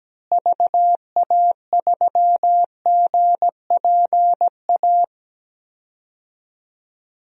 Morse code, heard as a keyed tone, about 17 wpm, 700 Hz. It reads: VA3GPA
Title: DXCC-100 Call Signs - 6 Characters - Repeated 2x - 17wpm